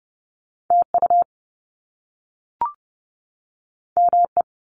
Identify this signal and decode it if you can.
Morse code, heard as a keyed tone, about 30 wpm, 700 Hz. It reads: TU ME
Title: Top Two Letter Combinations in Words 30wpm